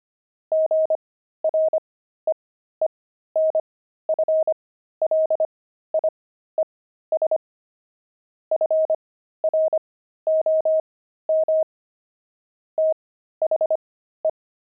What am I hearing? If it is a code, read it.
Morse code, heard as a keyed tone, about 25 wpm, spaced as for 13 wpm, 630 Hz. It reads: GREENFLIES FROM THE
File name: pangram-fs-13-cs-25-beehive